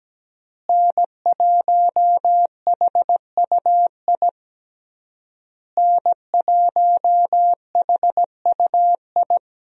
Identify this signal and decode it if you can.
Morse code, heard as a keyed tone, about 17 wpm, 700 Hz. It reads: N1HUI N1HUI
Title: USA Call Signs - 5 Characters - Repeated 3x - 17wpm